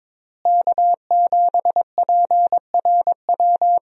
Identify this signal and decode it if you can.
Morse code, heard as a keyed tone, about 22 wpm, 700 Hz. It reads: K7PRW
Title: USA Call Signs - 5 Characters - Repeated 2x - 22wpm